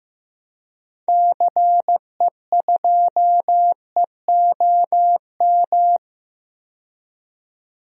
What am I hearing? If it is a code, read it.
Morse code, heard as a keyed tone, about 15 wpm, 700 Hz. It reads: CE2EOM
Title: DXCC-100 Call Signs - 6 Characters - 15wpm